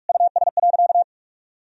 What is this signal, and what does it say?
Morse code, heard as a keyed tone, about 46 wpm, 700 Hz. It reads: US.
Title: Sentences from Top 1000 Words - Review Part 4 - 45wpm